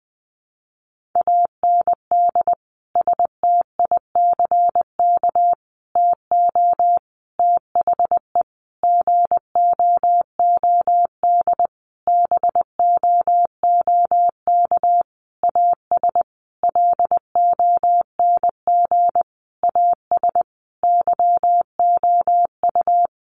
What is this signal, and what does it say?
Morse code, heard as a keyed tone, about 20 wpm, 700 Hz. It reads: AND STICK TO THE GOOD BOOK AS LONG AS YOU
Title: Sentences from Top 1200 Words 20wpm